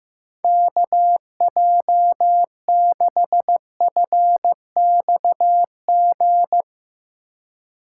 Morse code, heard as a keyed tone, about 15 wpm, 700 Hz. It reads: KJ6FXG
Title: Speed Racing - USA 2x3 Call Signs 15wpm